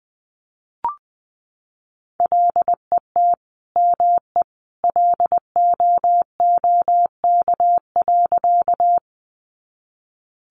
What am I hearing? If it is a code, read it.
Morse code, heard as a keyed tone, about 19 wpm, 700 Hz. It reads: LET ME LOOK.
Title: Sentences from Top 300 Words 20wpm